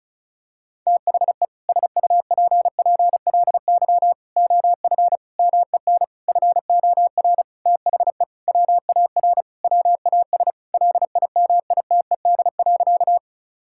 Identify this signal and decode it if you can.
Morse code, heard as a keyed tone, about 36 wpm, 700 Hz. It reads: THE SUPPLY OF MEN FOR THE WAR WAS LIMITED.
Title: Sentences from Top 1000 Words - Review Part 6 - 35wpm